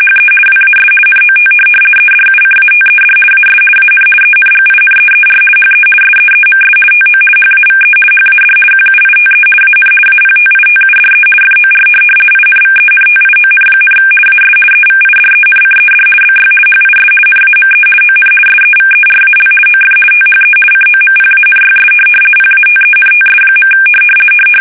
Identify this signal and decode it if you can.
RTTY, about 50 baud. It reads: HE WAS NOT EDUCATED AS AN ENGINEER HE SEEMS TO HAVE HAD QUITE AN APTITUDE FOR INVENTION, AS EVIDENCED BY HIS NAME ON SEVERAL OF THE COMPANYS PATENTS.
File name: fsk850-50B / teletype-corp